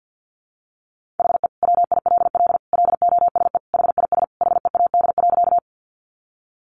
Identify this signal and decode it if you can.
Morse code, heard as a keyed tone, about 50 wpm, 700 Hz. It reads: HE WILL LOSE HIS HEAD.
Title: Sentences from Top 1100 Words 50wpm